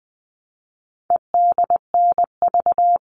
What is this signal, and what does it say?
Morse code, heard as a keyed tone, about 20 wpm, 700 Hz. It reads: EDNV
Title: All Cut Numbers - 4-Digit 20wpm